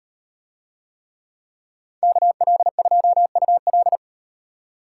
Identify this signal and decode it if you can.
Morse code, heard as a keyed tone, about 35 wpm, 700 Hz. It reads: KL2UL